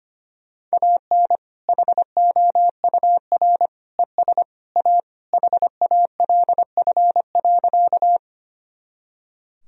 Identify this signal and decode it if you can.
Morse code, heard as a keyed tone, about 25 wpm, 700 Hz. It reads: AN HOUR ES A HALF.
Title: Sentences from Top 400 Words and Abbreviations 25wpm